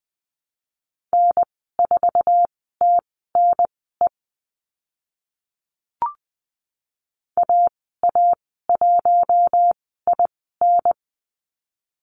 Morse code, heard as a keyed tone, about 20 wpm, spaced as for 14 wpm, 700 Hz. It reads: N4TNE AA1IN
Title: 055 Lesson 10g - Letter r - Call Signs 20wpm